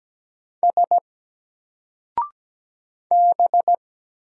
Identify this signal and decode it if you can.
Morse code, heard as a keyed tone, about 17 wpm, 700 Hz. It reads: S B
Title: Single Character (Letter:Number:Punctuation:Prosign) 17wpm